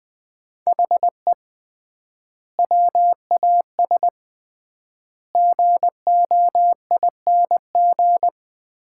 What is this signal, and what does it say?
Morse code, heard as a keyed tone, about 20 wpm, 700 Hz. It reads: HE WAS GOING
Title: 3x Word-Spacing - Sets of 4 Words - 20wpm